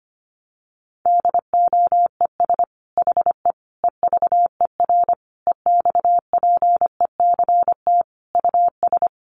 Morse code, heard as a keyed tone, about 25 wpm, 700 Hz. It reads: DOES HE EVER EXPECT US
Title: Sentences from Top 1100 Words 25wpm